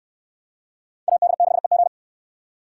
Morse code, heard as a keyed tone, about 62 wpm, 700 Hz. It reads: AD6EG